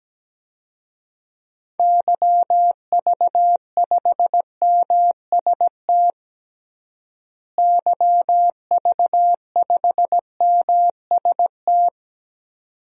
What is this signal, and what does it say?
Morse code, heard as a keyed tone, about 17 wpm, 700 Hz. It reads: YV5MST YV5MST
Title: DXCC-100 Call Signs - 6 Characters - Repeated 2x - 17wpm